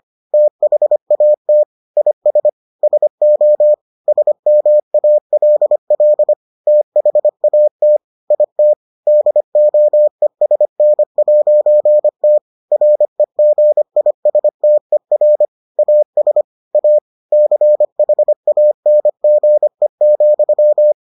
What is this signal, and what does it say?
Morse code, heard as a keyed tone, about 25 wpm, 600 Hz. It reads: THAT IS SO SMALL THAT IT DOESN'T REGISTER AS A CHANGE,